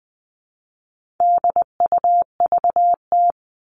Morse code, heard as a keyed tone, about 20 wpm, 700 Hz. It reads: DUVT
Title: All Cut Numbers - 4-Digit 20wpm